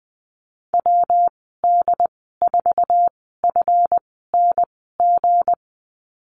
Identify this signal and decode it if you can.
Morse code, heard as a keyed tone, about 19 wpm, spaced as for 14 wpm, 700 Hz. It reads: WD4FNG
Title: 167 Lesson 27g - Number 7 - Call Signs 20wpm